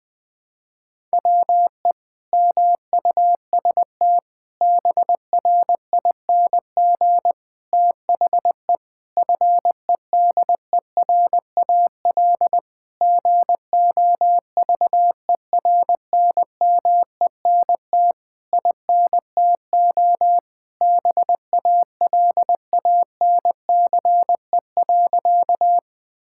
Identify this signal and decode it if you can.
Morse code, heard as a keyed tone, about 21 wpm, 700 Hz. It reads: WE MUST BRING THE FEDERAL GOVERNMENT INTO BALANCE.
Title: Sentences from Top 1000 Words - Review Part 6 - 20wpm